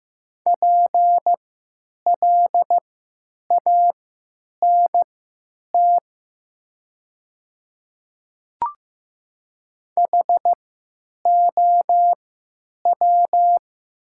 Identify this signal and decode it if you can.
Morse code, heard as a keyed tone, about 15 wpm, spaced as for 9 wpm, 700 Hz. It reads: PLANT HOW
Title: 3x Character Spacing - Top 300 Words - 15wpm